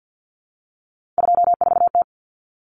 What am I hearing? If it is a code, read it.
Morse code, heard as a keyed tone, about 50 wpm, 700 Hz. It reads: 24T
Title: Common Cut Numbers - 3-Digit 50wpm